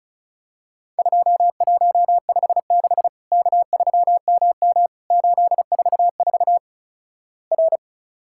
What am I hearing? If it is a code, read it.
Morse code, heard as a keyed tone, about 36 wpm, 700 Hz. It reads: 2156 K3MM 844 R
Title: Worked All Europe DX Contest - QTC Exchange - 35wpm